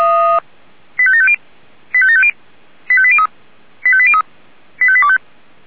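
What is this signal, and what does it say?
Five-tone selective call (ZVEI), heard as quick runs of five tones, tones of about 70 ms. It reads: 85750 85750 85803 85803 85825
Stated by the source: ZVEI five-tone selective call 85827 received by SDR.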